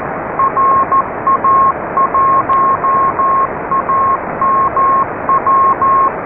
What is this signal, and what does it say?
Morse code, heard as a keyed tone, about 14 wpm, 1080 Hz, with noise. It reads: RA1AMW